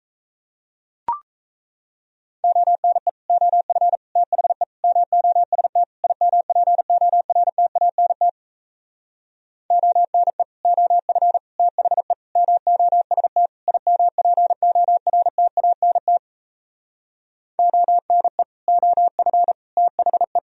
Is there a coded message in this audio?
Yes: Morse code from 2.4 to 20.4 s, heard as a keyed tone, about 37 wpm, 700 Hz. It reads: ONE OF THE MOST IMPORTANT ONE OF THE MOST IMPORTANT ONE OF THE